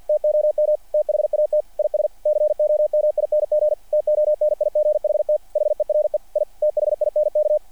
Morse code, heard as a keyed tone, about 48 wpm, 600 Hz, with radio noise. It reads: TOM THAT IS COMING TONIGHT HERE I THINK
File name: archived / CRSnet